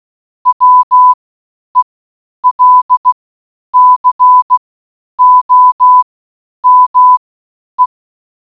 Morse code, heard as a keyed tone, about 16 wpm, spaced as for 9 wpm, 1000 Hz. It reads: WELCOME